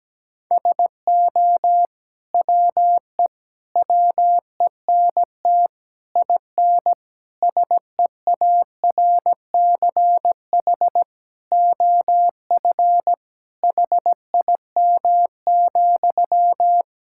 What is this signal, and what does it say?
Morse code, heard as a keyed tone, about 17 wpm, 700 Hz. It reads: SO WE WENT IN SEARCH OF HIM,
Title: Sentences from Top 1000 Words - Review Part 5 - 17wpm